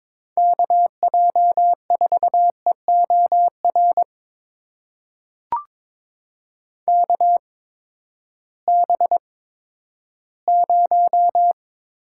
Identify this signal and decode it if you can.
Morse code, heard as a keyed tone, about 22 wpm, spaced as for 12 wpm, 700 Hz. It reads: KJ4EOR KB0
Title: USA Call Signs - 6 Characters - 8x Character Spacing - 22wpm